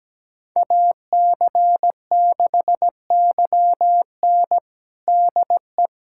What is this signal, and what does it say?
Morse code, heard as a keyed tone, about 17 wpm, 700 Hz. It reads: AC6YN DE
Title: USA Callsign DE Callsign - 5 Characters - 17wpm